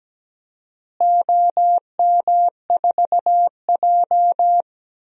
Morse code, heard as a keyed tone, about 17 wpm, 700 Hz. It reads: OM4J
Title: DXCC-100 Call Signs - 4 Characters - Repeated 3x - 17wpm